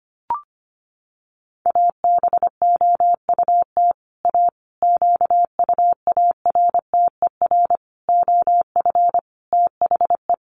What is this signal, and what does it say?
Morse code, heard as a keyed tone, about 25 wpm, 700 Hz. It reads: ABOUT A QUARTER OF THE
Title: Sentences from Top 1300 Words 25wpm